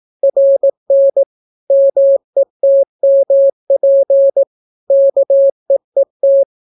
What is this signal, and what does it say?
Morse code, heard as a keyed tone, about 18 wpm, 550 Hz. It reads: RN METMP KEET